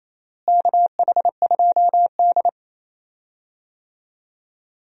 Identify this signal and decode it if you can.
Morse code, heard as a keyed tone, about 28 wpm, 700 Hz. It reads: KH2D
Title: USA Call Signs - 4 Characters - Repeated 2x - 28wpm